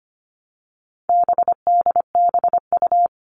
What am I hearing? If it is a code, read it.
Morse code, heard as a keyed tone, about 25 wpm, 700 Hz. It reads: BDBU
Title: All Cut Numbers - 4-Digit 25wpm